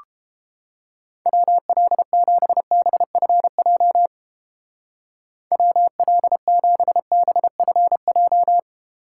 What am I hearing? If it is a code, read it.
Morse code, heard as a keyed tone, about 30 wpm, 700 Hz. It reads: WL7BFJ WL7BFJ